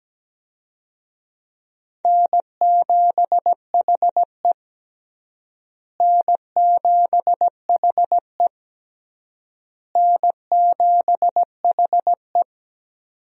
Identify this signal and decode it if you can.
Morse code, heard as a keyed tone, about 17 wpm, 700 Hz. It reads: N7HE N7HE N7HE
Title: USA Call Signs - 4 Characters - Repeated 3x - 17wpm